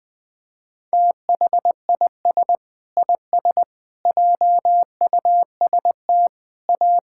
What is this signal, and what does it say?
Morse code, heard as a keyed tone, about 20 wpm, 700 Hz. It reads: THIS IS JUST A
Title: Sentences with 7-Word Phrases - 20wpm